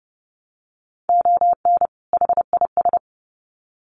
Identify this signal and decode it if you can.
Morse code, heard as a keyed tone, about 30 wpm, 700 Hz. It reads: ON HIS
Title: Sets of 2 Words 30wpm